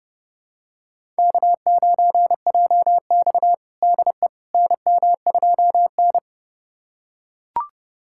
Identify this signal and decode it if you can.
Morse code, heard as a keyed tone, about 31 wpm, 700 Hz. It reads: K9JX DE NM2N
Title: USA Callsign DE Callsign - 4 Characters - 30wpm